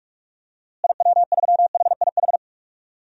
Morse code, heard as a keyed tone, about 42 wpm, 700 Hz. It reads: IW3HIH
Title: DXCC-100 Call Signs - 6 Characters - Repeated 2x - 45wpm